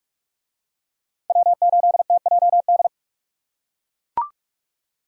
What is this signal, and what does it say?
Morse code, heard as a keyed tone, about 40 wpm, 700 Hz. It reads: W8TJD